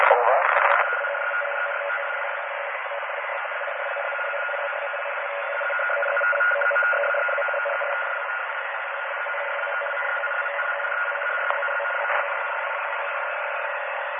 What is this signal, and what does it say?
Two-tone teleprinter signal, about 850 Hz shift, with speech.